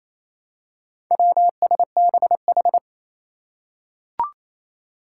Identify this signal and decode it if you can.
Morse code, heard as a keyed tone, about 28 wpm, 700 Hz. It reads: WSBH
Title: Four Challenging Letter-Number Combinations 28wpm